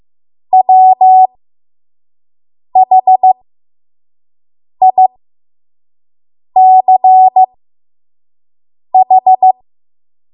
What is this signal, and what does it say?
Morse code, heard as a keyed tone, about 15 wpm, spaced as for 5 wpm, 750 Hz. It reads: WHICH
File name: Morse / 05WPM